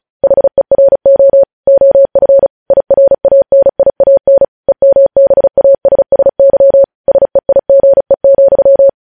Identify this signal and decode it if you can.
Morse code, heard as a keyed tone, about 36 wpm, 550 Hz. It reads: HERO OF IRANIAN EMBASSY SEIGE,
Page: news headlines in morse code at 35 wpm